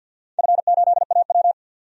Morse code, heard as a keyed tone, about 48 wpm, 700 Hz. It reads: F8AW
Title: DXCC-100 Call Signs - 4 Characters - 50wpm